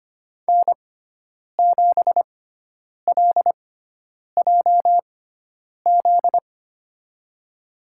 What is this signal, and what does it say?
Morse code, heard as a keyed tone, about 25 wpm, spaced as for 9 wpm, 700 Hz. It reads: N7LJZ
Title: USA Call Signs - 6x Character Spacing - 25wpm